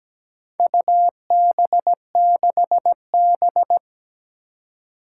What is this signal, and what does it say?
Morse code, heard as a keyed tone, about 17 wpm, 700 Hz. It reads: UB6B
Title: Four Challenging Letter-Number Combinations 17wpm